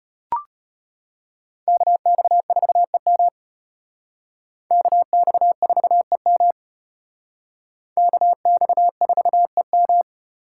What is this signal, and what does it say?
Morse code, heard as a keyed tone, about 33 wpm, 700 Hz. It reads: KX4EM KX4EM KX4EM